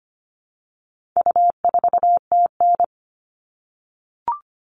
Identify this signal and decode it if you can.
Morse code, heard as a keyed tone, about 25 wpm, 700 Hz. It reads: U4TN
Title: All Cut Numbers - 4-Digit 25wpm